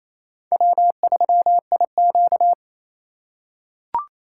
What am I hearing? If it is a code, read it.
Morse code, heard as a keyed tone, about 27 wpm, 700 Hz. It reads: W3IQ
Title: USA Call Signs - 4x Character Spacing - 28wpm